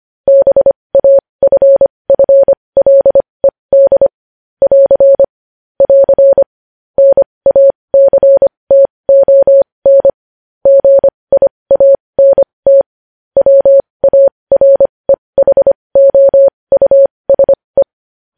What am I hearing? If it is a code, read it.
Morse code, heard as a keyed tone, about 25 wpm, spaced as for 20 wpm, 550 Hz. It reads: BAFFLED <AR> <AR> NACTON GIANT WAREHOUSE
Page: news-headlines-in-morse-code-at-20-wpm